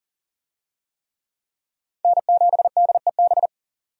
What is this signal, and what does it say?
Morse code, heard as a keyed tone, about 40 wpm, 700 Hz. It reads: N7DEB
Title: USA Call Signs - 5 Characters - 40wpm